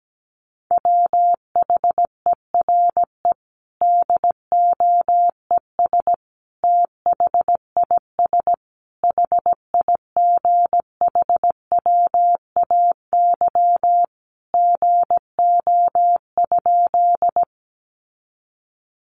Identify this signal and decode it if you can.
Morse code, heard as a keyed tone, about 17 wpm, 700 Hz. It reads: WHERE DOES THIS HIGHWAY GO?
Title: Sentences from Top 1400 Words 17wpm